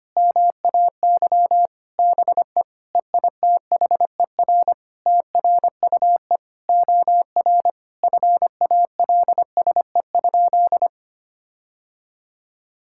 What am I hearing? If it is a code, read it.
Morse code, heard as a keyed tone, about 25 wpm, 700 Hz. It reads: MAY BE EITHER TRUE OR FALSE?
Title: Sentences from Top 1600 Words 25wpm